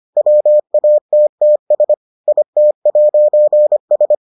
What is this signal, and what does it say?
Morse code, heard as a keyed tone, about 25 wpm, 600 Hz. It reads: WATTS IT'S